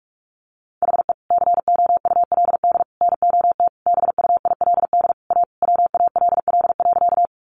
Morse code, heard as a keyed tone, about 44 wpm, 700 Hz. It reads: HE COULD NOT BUILD A WALL.
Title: Sentences from Top 1100 Words 45wpm